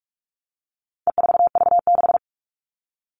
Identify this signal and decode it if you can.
Morse code, heard as a keyed tone, about 44 wpm, 700 Hz. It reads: E4V6
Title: All Cut Numbers - 4-Digit 45wpm